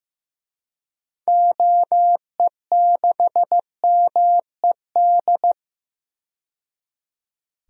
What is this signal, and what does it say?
Morse code, heard as a keyed tone, about 15 wpm, 700 Hz. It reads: OE6MED
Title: DXCC-100 Call Signs - 6 Characters - 15wpm